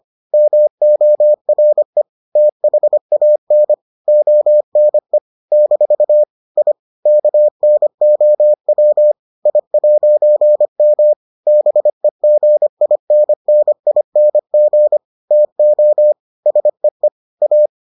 Morse code, heard as a keyed tone, about 25 wpm, 600 Hz. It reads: MORE THAN ONE - I KNOW I'M BEGINNING TO SEE A